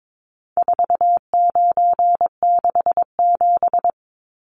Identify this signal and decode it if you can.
Morse code, heard as a keyed tone, about 22 wpm, 700 Hz. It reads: 4967
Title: Numbers - 4-Digit 22wpm